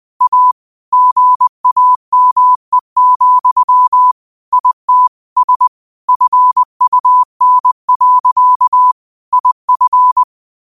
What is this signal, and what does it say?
Morse code, heard as a keyed tone, about 20 wpm, 1000 Hz. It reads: A GAME, IT S FUN. IF
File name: qotdfiles / 20wpm